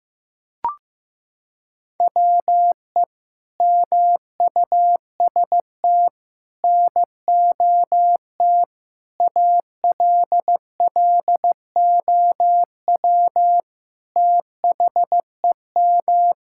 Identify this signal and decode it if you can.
Morse code, heard as a keyed tone, about 15 wpm, 700 Hz. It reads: WE MUST NOT ALLOW THEM
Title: Sentences from Top 1000 Words - Review Part 5 - 15wpm